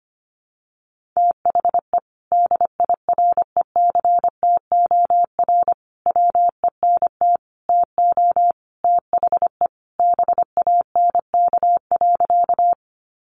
Morse code, heard as a keyed tone, about 25 wpm, 700 Hz. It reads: THE DIRECTOR WENT TO THE BANK.
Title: Sentences from Top 1300 Words 25wpm